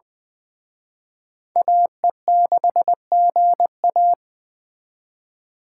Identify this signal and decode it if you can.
Morse code, heard as a keyed tone, about 24 wpm, 700 Hz. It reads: AE6GA
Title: Speed Racing - USA 2x2 Call Signs 20wpm